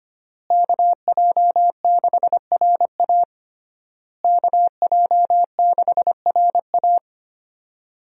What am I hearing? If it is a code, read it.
Morse code, heard as a keyed tone, about 25 wpm, 700 Hz. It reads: KJ6RA KJ6RA
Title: USA Call Signs - 5 Characters - Repeated 3x - 25wpm